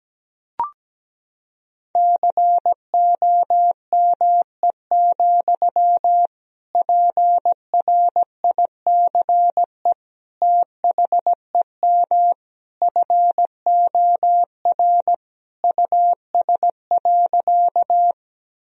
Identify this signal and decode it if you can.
Morse code, heard as a keyed tone, about 17 wpm, 700 Hz. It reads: COME, PRICE THEM FOR US.
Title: Sentences from Top 1000 Words - Review Part 4 - 17wpm